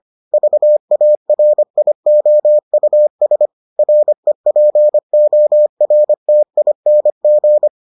Morse code, heard as a keyed tone, about 25 wpm, 600 Hz. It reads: VARIOUS REPORTING